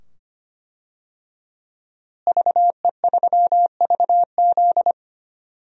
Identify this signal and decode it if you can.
Morse code, heard as a keyed tone, about 25 wpm, 700 Hz. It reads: VE3VZ
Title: DXCC-100 Call Signs - 5 Characters - Repeated 2x - 25wpm